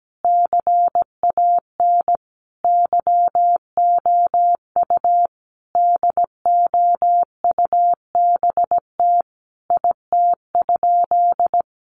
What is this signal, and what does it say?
Morse code, heard as a keyed tone, about 17 wpm, 700 Hz. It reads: CAN YOU DOUBT IT?
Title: Sentences from Top 1500 Words 17wpm